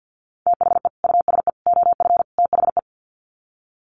Morse code, heard as a keyed tone, about 48 wpm, 700 Hz. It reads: THE USE OF THE
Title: Sets of 4 Words 50wpm